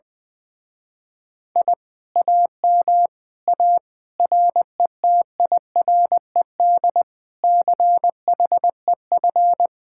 Morse code, heard as a keyed tone, about 20 wpm, 700 Hz. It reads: I AM A RETIRED CHEF
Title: Speed-Racing Common QSO Phrases - 20wpm